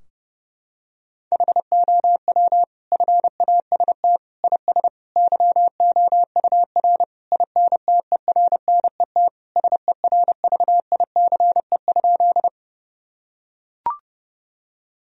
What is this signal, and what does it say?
Morse code, heard as a keyed tone, about 32 wpm, 700 Hz. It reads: HOW FAST IS YOUR INTERNET SERVICE?
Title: Sentences from Top 1000 Words - Review Part 4 - 30wpm